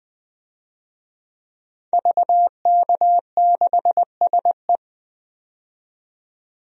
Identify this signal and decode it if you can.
Morse code, heard as a keyed tone, about 20 wpm, 700 Hz. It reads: VK6SE
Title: DXCC-100 Call Signs - 5 Characters - 20wpm